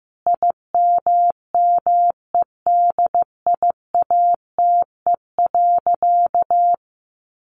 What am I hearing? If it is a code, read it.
Morse code, heard as a keyed tone, about 15 wpm, 700 Hz. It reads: IMMEDIATE.
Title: Sentences from Top 1300 Words 15wpm